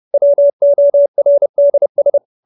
Morse code, heard as a keyed tone, about 30 wpm, 570 Hz. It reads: WORDS
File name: QSO317 30